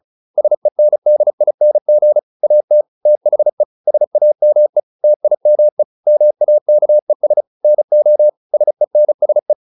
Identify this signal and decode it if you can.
Morse code, heard as a keyed tone, about 34 wpm, 600 Hz. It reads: SENDING AT THE SAME TIME MAKES NO SENSE